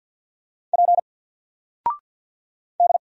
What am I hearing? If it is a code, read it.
Morse code, heard as a keyed tone, about 50 wpm, 700 Hz. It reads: P D
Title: Single Character (Letter:Number:Punctuation:Prosign) 50wpm